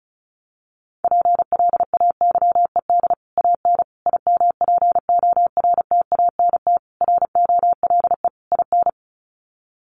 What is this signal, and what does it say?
Morse code, heard as a keyed tone, about 34 wpm, 700 Hz. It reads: PLAYED AN IMPORTANT ROLE IN
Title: Sets of 5 Words Encore2 35wpm